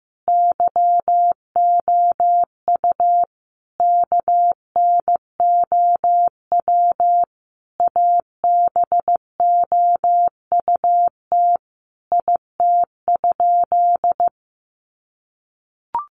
Morse code, heard as a keyed tone, about 15 wpm, 700 Hz. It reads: YOU KNOW ABOUT IT?
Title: Sentences from Top 100 Words 15wpm